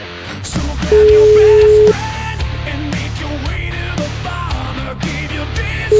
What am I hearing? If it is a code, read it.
Ringback tone (Europe, 425 Hz).